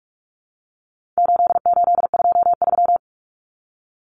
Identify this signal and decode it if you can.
Morse code, heard as a keyed tone, about 44 wpm, 700 Hz. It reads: 8823
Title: Numbers - 4-Digit 45wpm